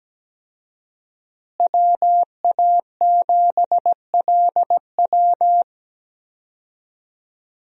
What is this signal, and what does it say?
Morse code, heard as a keyed tone, about 17 wpm, 700 Hz. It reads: WA7LW
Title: USA Call Signs - 5 Characters - Repeated 3x - 17wpm